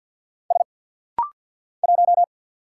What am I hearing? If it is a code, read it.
Morse code, heard as a keyed tone, about 50 wpm, 700 Hz. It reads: S 1
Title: Single Letter-Number Alt - Rapid-Fire 50wpm